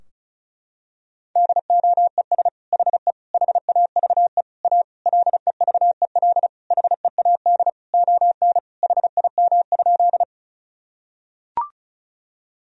Morse code, heard as a keyed tone, about 36 wpm, 700 Hz. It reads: DOES HE HAVE A LEVEL HEAD ON HIM?
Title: Sentences from Top 1000 Words - Review Part 4 - 35wpm